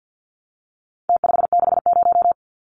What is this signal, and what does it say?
Morse code, heard as a keyed tone, about 48 wpm, 700 Hz. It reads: T560
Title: Numbers - Contest Style - 4-Digit 50wpm